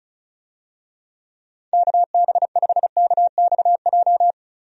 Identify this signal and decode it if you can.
Morse code, heard as a keyed tone, about 34 wpm, 700 Hz. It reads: KB5KXJ
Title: USA Call Signs - 6 Characters - Repeated 3x - 35wpm